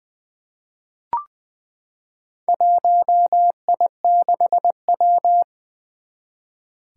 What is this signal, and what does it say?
Morse code, heard as a keyed tone, about 21 wpm, 700 Hz. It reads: 1I6W
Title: Four Challenging Letter-Number Combinations 20wpm